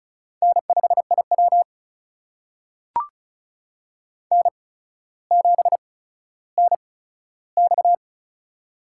Morse code, heard as a keyed tone, about 36 wpm, spaced as for 15 wpm, 700 Hz. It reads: N5IW N7NX
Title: USA Call Signs - 4 Characters - 8x Character Spacing - 35wpm